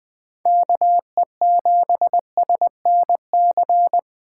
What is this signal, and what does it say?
Morse code, heard as a keyed tone, about 21 wpm, 700 Hz. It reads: KE7SNC
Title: USA Call Signs - 6 Characters - 3x Character Spacing - 20wpm